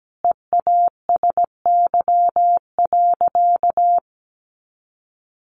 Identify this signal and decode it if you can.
Morse code, heard as a keyed tone, about 17 wpm, 700 Hz. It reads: EASY.
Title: Sentences from Top 1500 Words 17wpm